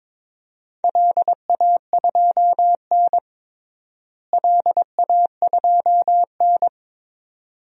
Morse code, heard as a keyed tone, about 22 wpm, 700 Hz. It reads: LA2N LA2N
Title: DXCC-100 Call Signs - 4 Characters - Repeated 2x - 22wpm